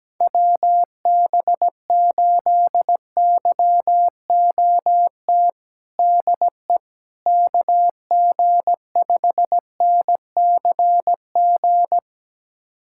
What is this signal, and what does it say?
Morse code, heard as a keyed tone, about 17 wpm, 700 Hz. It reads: WB8YOT DE KG5NCG
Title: USA Callsign DE Callsign 17wpm